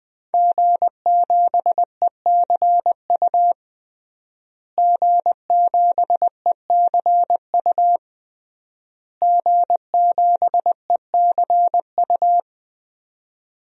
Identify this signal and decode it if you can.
Morse code, heard as a keyed tone, about 21 wpm, 700 Hz. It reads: G7ECU G7ECU G7ECU
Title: DXCC-100 Call Signs - 5 Characters - Repeated 3x - 20wpm